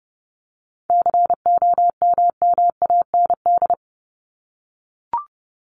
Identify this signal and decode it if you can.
Morse code, heard as a keyed tone, about 30 wpm, 700 Hz. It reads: COMMAND
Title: Top 1501-1750 Words 30wpm